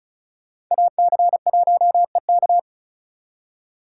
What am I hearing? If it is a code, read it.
Morse code, heard as a keyed tone, about 43 wpm, 700 Hz. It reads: AC1EK
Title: Speed Racing - USA Call Signs 35wpm